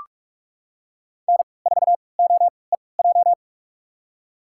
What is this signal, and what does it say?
Morse code, heard as a keyed tone, about 46 wpm, spaced as for 26 wpm, 700 Hz. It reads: N4OEJ
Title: USA Call Signs - 5 Characters - 3x Character Spacing - 45wpm